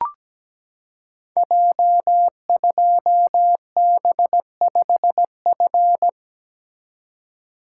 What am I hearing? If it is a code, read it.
Morse code, heard as a keyed tone, about 17 wpm, 700 Hz. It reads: J2B5F
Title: Code Groups - 5 Letter:Numbers - 17wpm